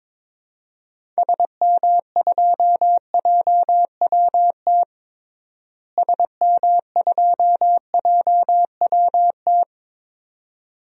Morse code, heard as a keyed tone, about 22 wpm, 700 Hz. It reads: SM2JWT SM2JWT
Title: DXCC-100 Call Signs - 6 Characters - Repeated 2x - 22wpm